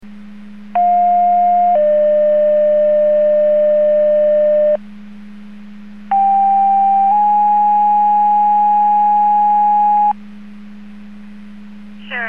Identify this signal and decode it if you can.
Two-tone page, heard as a one-second tone, then a three-second tone, about 710 then 600 Hz, about 790 then 830 Hz. It is heard from 0.8 to 10.1 s.